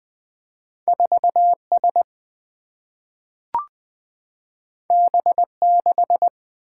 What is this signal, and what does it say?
Morse code, heard as a keyed tone, about 20 wpm, 700 Hz. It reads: 4 S B 6
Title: Single-Letter Single-Number 20wpm